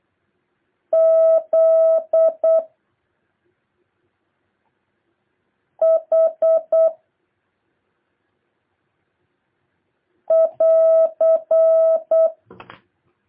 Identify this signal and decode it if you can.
Morse code, heard as a keyed tone, about 8 wpm, 650 Hz. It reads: Z H <AR>
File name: jh8cbh / 2023cw